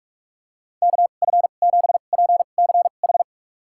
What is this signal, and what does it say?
Morse code, heard as a keyed tone, about 45 wpm, spaced as for 32 wpm, 700 Hz. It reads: KF7PCH
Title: USA Call Signs - 6 Characters - 2x Character Spacing - 45wpm